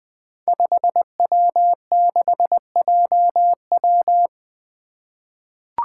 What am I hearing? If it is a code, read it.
Morse code, heard as a keyed tone, about 20 wpm, 700 Hz. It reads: 5W6JW
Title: Five Challenging Letter-Number Combinations 20wpm